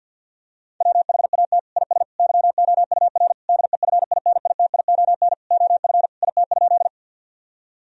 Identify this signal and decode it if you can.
Morse code, heard as a keyed tone, about 50 wpm, 700 Hz. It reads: WHAT IS YOUR DEFINITION OF IT?
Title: Sentences from Top 1000 Words - Review Part 4 - 50wpm